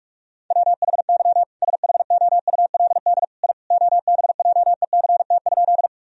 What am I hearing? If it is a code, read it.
Morse code, heard as a keyed tone, about 42 wpm, 700 Hz. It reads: WHY SHOULD I OBJECT?
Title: Sentences from Top 1000 Words - Review Part 3 - 45wpm